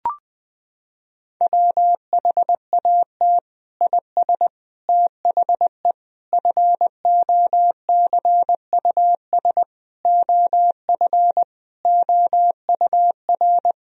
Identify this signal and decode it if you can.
Morse code, heard as a keyed tone, about 20 wpm, 700 Hz. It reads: WHAT IS THE FOCUS OF OUR
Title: Sentences from Top 1000 Words - Review Part 4 - 20wpm